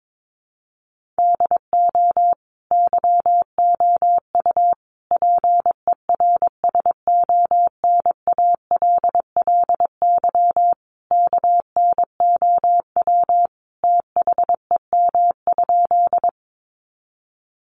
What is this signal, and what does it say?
Morse code, heard as a keyed tone, about 22 wpm, 700 Hz. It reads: DO YOU PERSONALLY KNOW THEM?
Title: Sentences from Top 1400 Words 22wpm